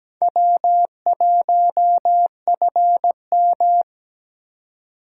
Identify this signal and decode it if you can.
Morse code, heard as a keyed tone, about 17 wpm, 700 Hz. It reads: W1FM
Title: USA Call Signs - 4 Characters - 2x Character Spacing - 17wpm